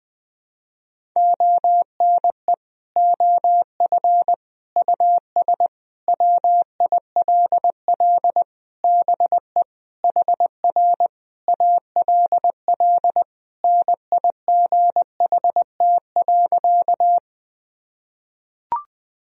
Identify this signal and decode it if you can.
Morse code, heard as a keyed tone, about 20 wpm, 700 Hz. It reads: ONE OF US WILL BE HR ALL NIGHT.
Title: Sentences from Top 300 Words and Abbreviations 20wpm